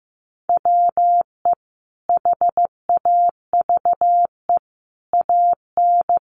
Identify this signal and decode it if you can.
Morse code, heard as a keyed tone, about 15 wpm, 700 Hz. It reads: WE HAVE AN
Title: Sentences from Top 800 Words 15wpm